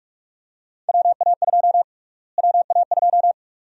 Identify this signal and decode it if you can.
Morse code, heard as a keyed tone, about 46 wpm, 700 Hz. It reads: WA2 WA2
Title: USA Call Sign Prefixes - Repeated 2x - 45wpm